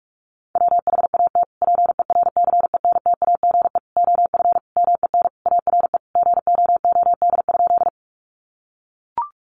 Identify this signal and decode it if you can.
Morse code, heard as a keyed tone, about 44 wpm, 700 Hz. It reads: WHAT PERCENTAGE OF MEN ARE GOOD?
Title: Sentences from Top 1300 Words 45wpm